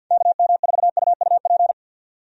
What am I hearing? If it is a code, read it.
Morse code, heard as a keyed tone, about 49 wpm, 700 Hz. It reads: KM4UUP
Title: USA Call Signs - 6 Characters - 50wpm